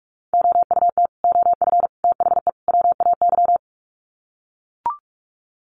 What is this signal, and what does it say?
Morse code, heard as a keyed tone, about 44 wpm, 700 Hz. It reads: OUT OF THE WAY
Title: Sets of 4 Words 45wpm